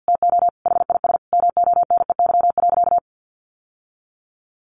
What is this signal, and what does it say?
Morse code, heard as a keyed tone, about 50 wpm, 700 Hz. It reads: TO HIS MONEY.
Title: Sentences from Top 1200 Words 50wpm